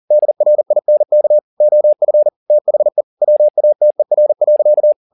Morse code, heard as a keyed tone, about 40 wpm, 600 Hz. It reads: DRINK OF THE WATER.
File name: the old man and the sea 2 40wpm